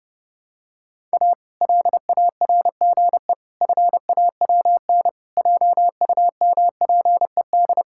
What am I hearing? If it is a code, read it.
Morse code, heard as a keyed tone, about 32 wpm, 700 Hz. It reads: A LARGE FAWN JUMPED
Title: Pangram Challenge v2 - 30wpm